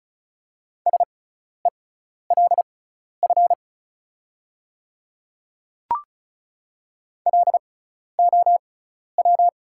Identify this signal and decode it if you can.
Morse code, heard as a keyed tone, about 36 wpm, spaced as for 12 wpm, 690 Hz. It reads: SELF LOW
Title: 6x Character Spacing - Top 500 Words - 35wpm